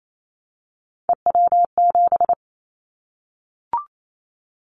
Morse code, heard as a keyed tone, about 28 wpm, 700 Hz. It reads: EW7
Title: Callsign Prefixes for Top 50 DXCC - 28wpm